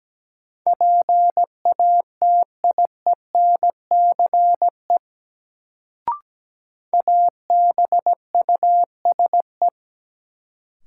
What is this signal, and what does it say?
Morse code, heard as a keyed tone, about 17 wpm, 700 Hz. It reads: PATIENCE ABUSE
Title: Sentences from Top 1300 Words - New Word Drill Alt - 17wpm